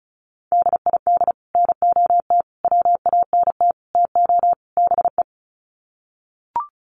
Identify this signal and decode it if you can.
Morse code, heard as a keyed tone, about 34 wpm, 700 Hz. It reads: DID NOT WANT TO BE
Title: Sets of 5 Words Encore 35wpm